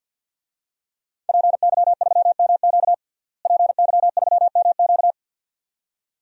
Morse code, heard as a keyed tone, about 52 wpm, 700 Hz. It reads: PY3MQ PY3MQ
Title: DXCC-100 Call Signs - 5 Characters - Repeated 2x - 50wpm